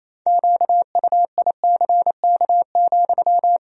Morse code, heard as a keyed tone, about 28 wpm, 700 Hz. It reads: QUICK,
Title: 4x Word-Spacing - Sentences from Top 500 Words - 28wpm